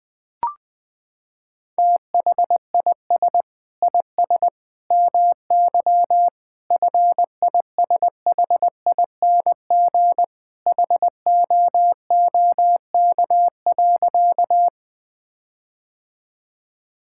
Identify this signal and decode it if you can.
Morse code, heard as a keyed tone, about 20 wpm, 700 Hz. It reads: THIS IS MY FISHING HOOK.
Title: Sentences from Top 1600 Words 20wpm